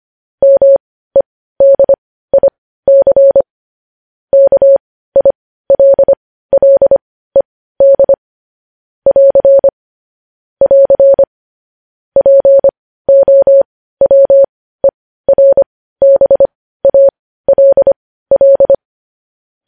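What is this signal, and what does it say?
Morse code, heard as a keyed tone, about 25 wpm, spaced as for 15 wpm, 550 Hz. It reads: MEDIC KILLED <AR> <AR> POWERBALL
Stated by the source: Morse code news headlines sent at 15.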